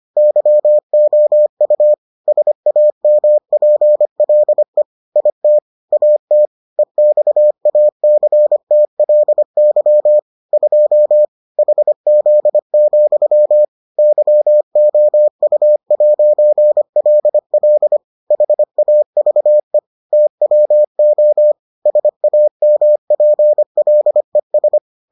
Morse code, heard as a keyed tone, about 25 wpm, 600 Hz. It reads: YOU SAMPLE IT AT EXACTLY 2 HZ, YOU'LL HAVE TWO SAMPLES